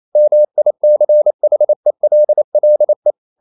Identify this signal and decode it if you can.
Morse code, heard as a keyed tone, about 28 wpm, 600 Hz. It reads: MICHELLE
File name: Ferie QRQ 28wpm